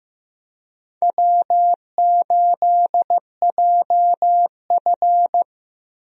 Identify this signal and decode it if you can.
Morse code, heard as a keyed tone, about 15 wpm, 700 Hz. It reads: W8JF
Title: USA Call Signs - 4 Characters - Repeated 2x - 15wpm